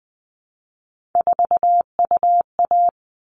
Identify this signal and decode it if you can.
Morse code, heard as a keyed tone, about 20 wpm, 700 Hz. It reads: 4UA
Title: All Cut Numbers - 3-Digit 20wpm